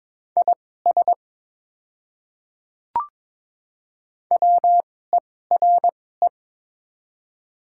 Morse code, heard as a keyed tone, about 22 wpm, spaced as for 16 wpm, 690 Hz. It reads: IS WERE
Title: 2x Character Spacing - Top 100 Words - 22wpm